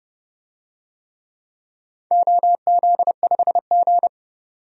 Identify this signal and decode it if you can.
Morse code, heard as a keyed tone, about 30 wpm, 700 Hz. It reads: OZ5G
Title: DXCC-100 Call Signs - 4 Characters - 30wpm